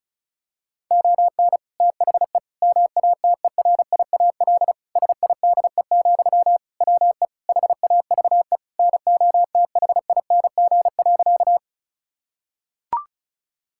Morse code, heard as a keyed tone, about 34 wpm, 700 Hz. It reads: ON THE MATERIAL SIDE, WE HAVE NOTHING.
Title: Sentences from Top 1000 Words - Review Part 5 - 35wpm